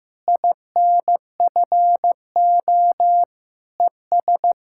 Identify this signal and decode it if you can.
Morse code, heard as a keyed tone, about 15 wpm, 700 Hz. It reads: INFO ES
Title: Practice QSO - Exchange 4 - Ending - 15wpm